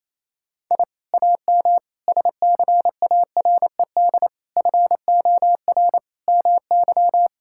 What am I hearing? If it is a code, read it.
Morse code, heard as a keyed tone, about 28 wpm, 700 Hz. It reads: I AM SCARED FOR MY
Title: Sentences from Top 1000 Words - Review Part 5 - 28wpm